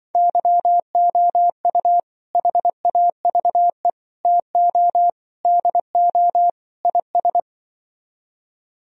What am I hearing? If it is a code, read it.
Morse code, heard as a keyed tone, about 25 wpm, 700 Hz. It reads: YOU HAVE TO DO IS